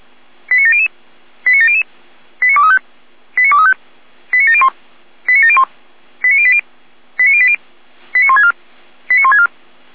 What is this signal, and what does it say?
Five-tone selective call (ZVEI), heard as quick runs of five tones, tones of about 70 ms. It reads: 89700 89700 89235 89235 89701 89701 89080 89080 89164 89164